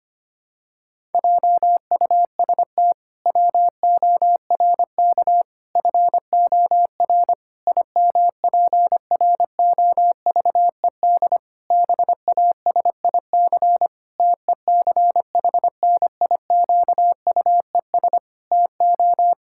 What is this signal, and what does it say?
Morse code, heard as a keyed tone, about 25 wpm, 700 Hz. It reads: JUST WORK FOR IMPROVED BASIC TECHNIQUES TO
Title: Pangram Challenge v2 - 25wpm